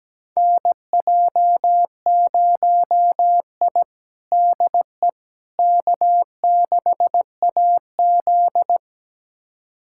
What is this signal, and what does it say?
Morse code, heard as a keyed tone, about 17 wpm, 700 Hz. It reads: NJ0I DE K6AZ
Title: USA Callsign DE Callsign - 4 Characters - 17wpm